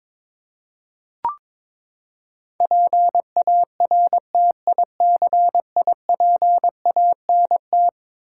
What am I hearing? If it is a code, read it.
Morse code, heard as a keyed tone, about 22 wpm, 700 Hz. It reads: PARTICIPANT
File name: practice / 22wpm / Eleven Letter Words - 22wpm